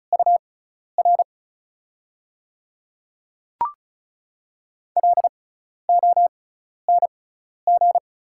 Morse code, heard as a keyed tone, about 36 wpm, spaced as for 12 wpm, 690 Hz. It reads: UR LONG
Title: 6x Character Spacing - Top 100 Words - 35wpm